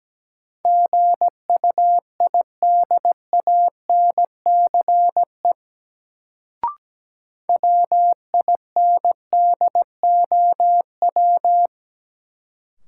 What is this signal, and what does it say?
Morse code, heard as a keyed tone, about 17 wpm, 700 Hz. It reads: GUIDANCE WINDOW
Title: Sentences from Top 1600 Words - New Word Drill Alt - 17wpm